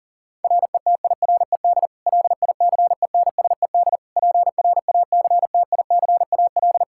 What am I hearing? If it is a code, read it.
Morse code, heard as a keyed tone, about 37 wpm, 700 Hz. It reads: RETIRED LICENSED PRACTICAL
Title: Practice QSO - Exchange 3 - More Info - 40wpm